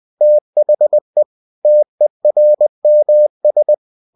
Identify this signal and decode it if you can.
Morse code, heard as a keyed tone, about 20 wpm, 600 Hz. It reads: THE TERMS
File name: cw-books / winnie-the-pooh / winnie-20wpm-page-0117